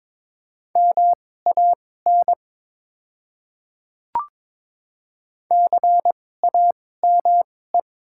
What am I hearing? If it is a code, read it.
Morse code, heard as a keyed tone, about 22 wpm, spaced as for 16 wpm, 700 Hz. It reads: MAN CAME
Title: 2x Character Spacing - Top 200 Words - 22wpm